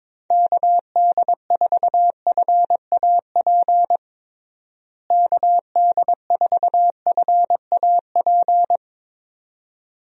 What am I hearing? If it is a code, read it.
Morse code, heard as a keyed tone, about 23 wpm, 700 Hz. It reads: KD4FAP KD4FAP
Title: USA Call Signs - 6 Characters - Repeated 2x - 22wpm